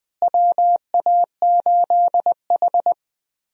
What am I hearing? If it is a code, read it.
Morse code, heard as a keyed tone, about 20 wpm, 700 Hz. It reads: WA8H
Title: USA Call Signs - 4 Characters - Repeated 3x - 20wpm